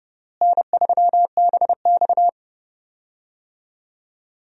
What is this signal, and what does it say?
Morse code, heard as a keyed tone, about 29 wpm, 700 Hz. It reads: N3BX
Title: USA Call Signs - 4 Characters - Repeated 2x - 30wpm